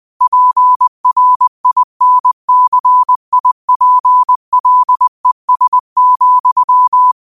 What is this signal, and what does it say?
Morse code, heard as a keyed tone, about 20 wpm, 1000 Hz. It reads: PRINCIPLES,
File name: qotdfiles / 20wpm